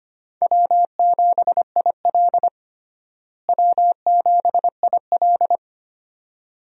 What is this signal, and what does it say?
Morse code, heard as a keyed tone, about 25 wpm, 700 Hz. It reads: W7IL W7IL
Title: USA Call Signs - 4 Characters - Repeated 2x - 25wpm